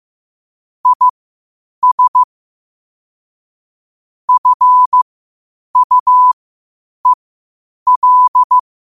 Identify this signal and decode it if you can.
Morse code, heard as a keyed tone, about 15 wpm, spaced as for 8 wpm, 1000 Hz. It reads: IS FUEL
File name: qotdfiles / 8wpm